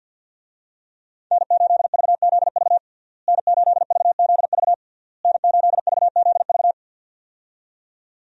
Morse code, heard as a keyed tone, about 52 wpm, 700 Hz. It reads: N8VZV N8VZV N8VZV
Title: USA Call Signs - 5 Characters - Repeated 3x - 50wpm